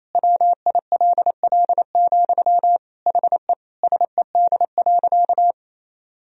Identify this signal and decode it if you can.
Morse code, heard as a keyed tone, about 28 wpm, 700 Hz. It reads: WILL, HE SED.
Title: Sentences from Top 200 Words and Abbreviations 28wpm